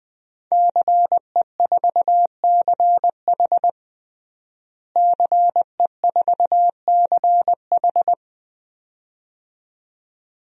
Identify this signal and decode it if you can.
Morse code, heard as a keyed tone, about 20 wpm, 700 Hz. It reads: CE4CH CE4CH
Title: DXCC-100 Call Signs - 5 Characters - Repeated 2x - 20wpm